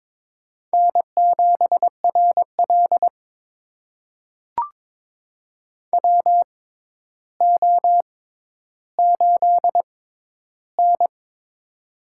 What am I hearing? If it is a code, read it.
Morse code, heard as a keyed tone, about 22 wpm, spaced as for 11 wpm, 700 Hz. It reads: N7RL WO8N
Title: USA Call Signs - 4 Characters - 6x Character Spacing - 22wpm